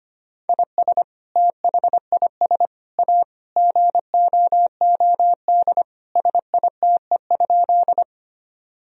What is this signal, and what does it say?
Morse code, heard as a keyed tone, about 25 wpm, 700 Hz. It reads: IS THIS A GOOD SITE?
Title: Sentences from Top 1000 Words - Review Part 4 - 25wpm